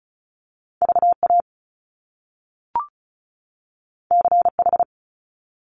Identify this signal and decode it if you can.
Morse code, heard as a keyed tone, about 36 wpm, 700 Hz. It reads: VA CH
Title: Top Two Letter Combinations in Words 35wpm